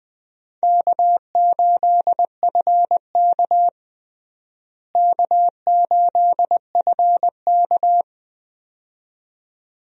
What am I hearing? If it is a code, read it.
Morse code, heard as a keyed tone, about 20 wpm, 700 Hz. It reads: K8FK K8FK
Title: USA Call Signs - 4 Characters - Repeated 2x - 20wpm